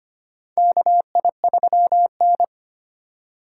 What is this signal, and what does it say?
Morse code, heard as a keyed tone, about 25 wpm, 700 Hz. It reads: KI3N
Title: USA Call Signs - 3x Character Spacing - 25wpm